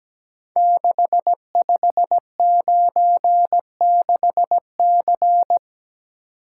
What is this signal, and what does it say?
Morse code, heard as a keyed tone, about 17 wpm, 700 Hz. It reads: 6596C
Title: Code Groups - 5 Letter:Numbers - 17wpm